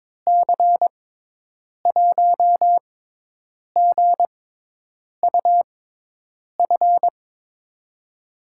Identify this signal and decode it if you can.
Morse code, heard as a keyed tone, about 22 wpm, spaced as for 8 wpm, 700 Hz. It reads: C1GUF
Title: USA Call Signs - 6 Characters - 6x Character Spacing - 22wpm